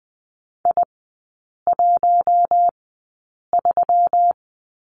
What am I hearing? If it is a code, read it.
Morse code, heard as a keyed tone, about 20 wpm, spaced as for 8 wpm, 700 Hz. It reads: I13
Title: 184 Lesson 30e - Letter b - Three Characters 20wpm